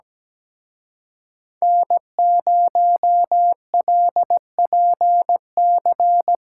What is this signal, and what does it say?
Morse code, heard as a keyed tone, about 20 wpm, 700 Hz. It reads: N0LPC
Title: Speed Racing - USA 1x3 Call Signs 17wpm